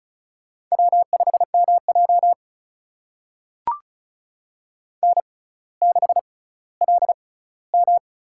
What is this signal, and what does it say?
Morse code, heard as a keyed tone, about 36 wpm, spaced as for 18 wpm, 700 Hz. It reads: W5MJ N6LM
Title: USA Call Signs - 4 Characters - 6x Character Spacing - 35wpm